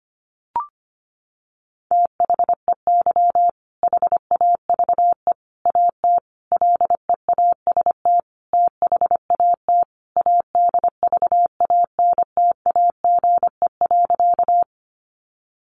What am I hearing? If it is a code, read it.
Morse code, heard as a keyed tone, about 25 wpm, 700 Hz. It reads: THEY HAVE AT LEAST THAT ADVANTAGE.
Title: Sentences from Top 900 Words 25wpm